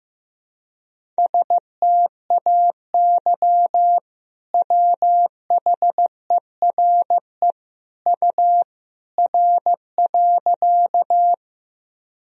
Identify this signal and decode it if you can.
Morse code, heard as a keyed tone, about 15 wpm, 700 Hz. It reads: STAY WHERE U R.
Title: Sentences from Top 500 Words and Abbreviations 15wpm